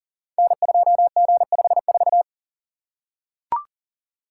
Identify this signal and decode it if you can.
Morse code, heard as a keyed tone, about 38 wpm, 700 Hz. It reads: N2G54
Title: Code Groups - 5 Letter:Numbers - 40wpm